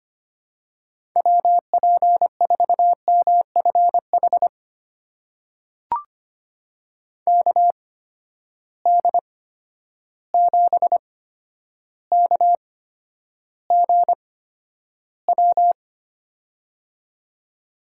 Morse code, heard as a keyed tone, about 25 wpm, spaced as for 11 wpm, 700 Hz. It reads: WP4MFH KD7KGW
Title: USA Call Signs - 8x Character Spacing - 25wpm